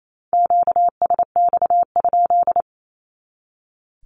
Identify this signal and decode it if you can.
Morse code, heard as a keyed tone, about 28 wpm, 700 Hz. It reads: QSX?
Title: Q-Codes 28wpm